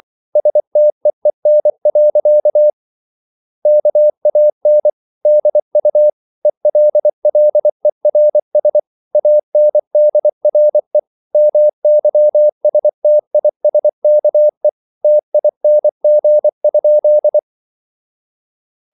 Morse code, heard as a keyed tone, about 24 wpm, 600 Hz. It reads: STEEN. KAN DU ELLERS ANDRE MYSTISKE TING?
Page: Ferie 24 wpm